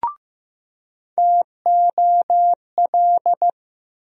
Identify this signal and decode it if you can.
Morse code, heard as a keyed tone, about 15 wpm, 700 Hz. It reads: TOL
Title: USA Call Sign Postfixes - 3 Letters 15wpm